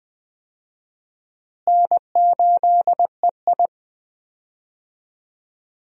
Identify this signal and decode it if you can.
Morse code, heard as a keyed tone, about 20 wpm, 700 Hz. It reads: N8EI
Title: USA Call Signs - 4 Characters - Repeated 2x - 20wpm